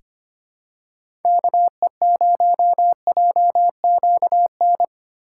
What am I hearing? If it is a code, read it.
Morse code, heard as a keyed tone, about 25 wpm, 700 Hz. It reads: KE0JQN
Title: USA Call Signs - 6 Characters - 4x Character Spacing - 25wpm